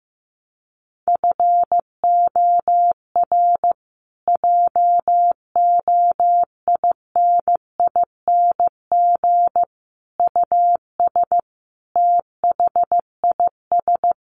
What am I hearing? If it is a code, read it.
Morse code, heard as a keyed tone, about 15 wpm, 700 Hz. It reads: FOR JOINING US THIS
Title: Sets of 5 Words Encore 15wpm